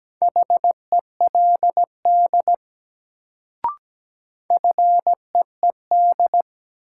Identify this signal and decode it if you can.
Morse code, heard as a keyed tone, about 17 wpm, 700 Hz. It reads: HELD FEED
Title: Sentences from Top 1600 Words - New Word Drill - 17wpm